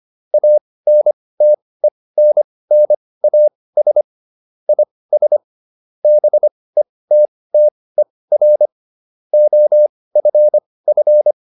Morse code, heard as a keyed tone, about 25 wpm, spaced as for 18 wpm, 600 Hz. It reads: ANTENNAS IS BETTER OFF